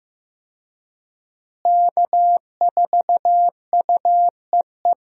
Morse code, heard as a keyed tone, about 15 wpm, 700 Hz. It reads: K4UEE
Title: USA Call Signs - 5 Characters - Repeated 3x - 15wpm